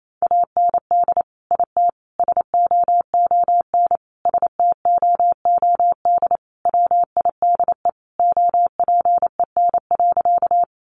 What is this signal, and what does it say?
Morse code, heard as a keyed tone, about 28 wpm, 700 Hz. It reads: AND IT SOON STOOD WIDE OPEN.
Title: Sentences from Top 900 Words 28wpm